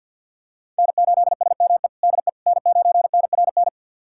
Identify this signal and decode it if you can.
Morse code, heard as a keyed tone, about 53 wpm, 700 Hz. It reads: N8SME DE N9NRN
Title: USA Callsign DE Callsign - 5 Characters - 50wpm